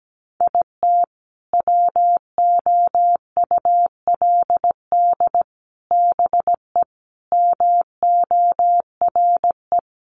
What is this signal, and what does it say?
Morse code, heard as a keyed tone, about 17 wpm, 700 Hz. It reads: IT WOULD BE MORE
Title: Sentences from Top 1000 Words 17wpm